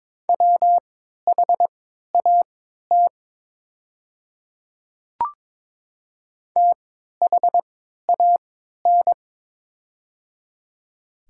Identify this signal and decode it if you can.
Morse code, heard as a keyed tone, about 22 wpm, spaced as for 13 wpm, 700 Hz. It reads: WHAT THAN
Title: 3x Character Spacing - Top 100 Words - 22wpm